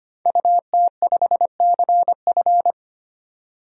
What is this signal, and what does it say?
Morse code, heard as a keyed tone, about 25 wpm, 700 Hz. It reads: UT5CF
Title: DXCC-100 Call Signs - 5 Characters - 25wpm